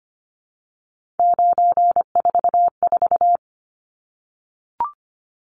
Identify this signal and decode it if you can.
Morse code, heard as a keyed tone, about 25 wpm, 700 Hz. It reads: 944
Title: Numbers - 3-Digit 25wpm